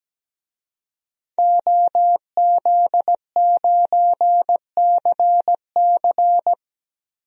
Morse code, heard as a keyed tone, about 17 wpm, 700 Hz. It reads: OZ9CC
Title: DXCC-100 Call Signs - 5 Characters - Repeated 2x - 17wpm